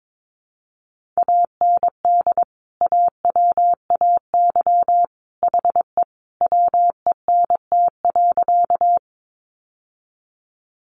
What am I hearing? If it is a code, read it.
Morse code, heard as a keyed tone, about 22 wpm, 700 Hz. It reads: AND AWAY HE WENT.
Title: Sentences from Top 600 Words 22wpm